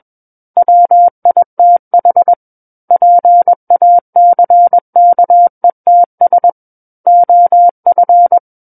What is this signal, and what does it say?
Morse code, heard as a keyed tone, about 21 wpm, 700 Hz. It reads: WITH PACKETS OF
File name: Book One / 12-Chapter 12 - 21 WPM